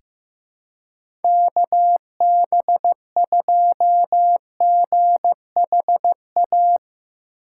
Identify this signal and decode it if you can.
Morse code, heard as a keyed tone, about 15 wpm, 700 Hz. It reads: KB2GHA
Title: USA Call Signs - 6 Characters - 3x Character Spacing - 15wpm